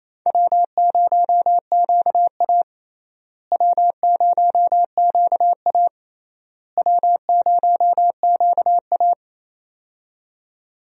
Morse code, heard as a keyed tone, about 28 wpm, 700 Hz. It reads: W0QA W0QA W0QA
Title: USA Call Signs - 4 Characters - Repeated 3x - 28wpm